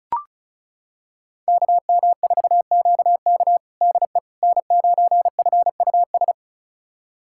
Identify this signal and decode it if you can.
Morse code, heard as a keyed tone, about 34 wpm, 700 Hz. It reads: KM4QK DE N9FUS
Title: USA Callsign DE Callsign - 5 Characters - 35wpm